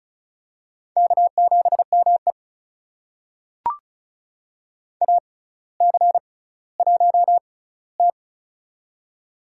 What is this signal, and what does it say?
Morse code, heard as a keyed tone, about 34 wpm, spaced as for 18 wpm, 700 Hz. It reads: K7ME AC1T
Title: USA Call Signs - 4 Characters - 6x Character Spacing - 35wpm